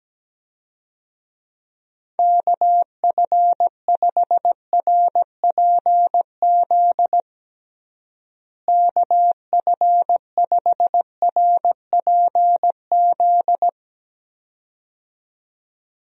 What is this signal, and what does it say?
Morse code, heard as a keyed tone, about 17 wpm, 700 Hz. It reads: KF5RPZ KF5RPZ
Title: USA Call Signs - 6 Characters - Repeated 2x - 17wpm